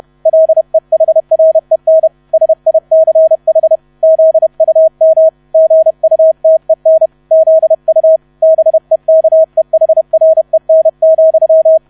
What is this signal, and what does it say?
Morse code, heard as a keyed tone, about 31 wpm, 640 Hz. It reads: LEHREN SICH ZUM GUTEN ZU BEKEHREN,
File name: 30wpm